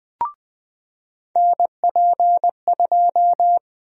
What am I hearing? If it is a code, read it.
Morse code, heard as a keyed tone, about 20 wpm, 700 Hz. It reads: NP2
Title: USA Call Sign Prefixes - 20wpm